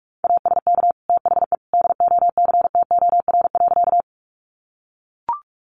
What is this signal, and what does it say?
Morse code, heard as a keyed tone, about 44 wpm, 700 Hz. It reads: ASK THE DOCTOR.
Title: Sentences from Top 1500 Words 45wpm